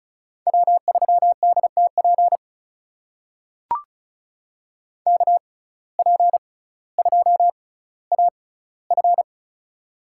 Morse code, heard as a keyed tone, about 34 wpm, spaced as for 18 wpm, 700 Hz. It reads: W3DTP KP2AF
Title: USA Call Signs - 5 Characters - 6x Character Spacing - 35wpm